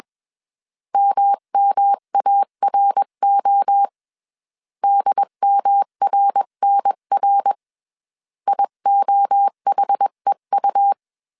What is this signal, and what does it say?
Morse code, heard as a keyed tone, about 21 wpm, 800 Hz. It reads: MMARO DMRNR IOHEU
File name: 20wpm